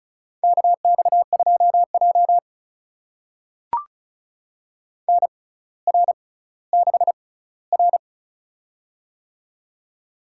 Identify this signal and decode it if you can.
Morse code, heard as a keyed tone, about 36 wpm, spaced as for 18 wpm, 700 Hz. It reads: KX2J NR6R
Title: USA Call Signs - 4 Characters - 6x Character Spacing - 35wpm